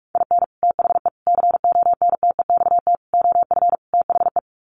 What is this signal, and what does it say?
Morse code, heard as a keyed tone, about 44 wpm, 700 Hz. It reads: IN THE CONTEXT OF THE
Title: Sets of 5 Words 45wpm